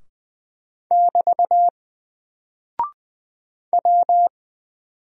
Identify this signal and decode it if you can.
Morse code, heard as a keyed tone, about 20 wpm, 700 Hz. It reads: <BT> W
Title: Single Character (Letter:Number:Punctuation:Prosign) Alt 20wpm